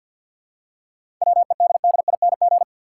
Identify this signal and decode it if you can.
Morse code, heard as a keyed tone, about 50 wpm, 700 Hz. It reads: WEDDING
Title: Seven Letter Words - 50wpm